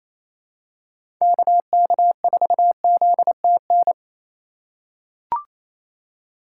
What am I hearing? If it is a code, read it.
Morse code, heard as a keyed tone, about 28 wpm, 700 Hz. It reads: KK4ZTN
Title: USA Call Signs - 4x Character Spacing - 28wpm